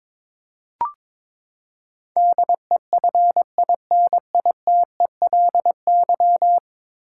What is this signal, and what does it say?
Morse code, heard as a keyed tone, about 22 wpm, 700 Hz. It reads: DEFINITELY
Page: morse-code-ninja-practice-22wpm